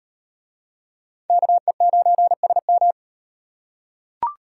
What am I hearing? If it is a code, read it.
Morse code, heard as a keyed tone, about 36 wpm, 700 Hz. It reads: KE9SM